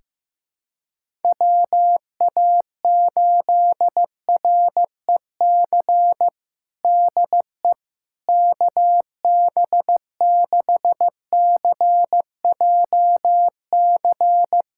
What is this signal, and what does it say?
Morse code, heard as a keyed tone, about 15 wpm, 700 Hz. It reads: WA8REC DE KB6CJC
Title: USA Callsign DE Callsign - 6 Characters - 15wpm